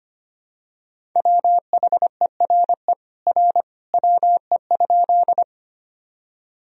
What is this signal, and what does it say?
Morse code, heard as a keyed tone, about 25 wpm, 700 Hz. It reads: WHERE R WE?
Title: Sentences from Top 200 Words and Abbreviations 25wpm